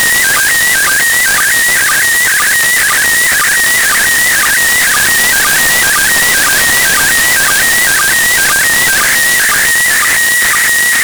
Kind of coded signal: SSTV (slow-scan television)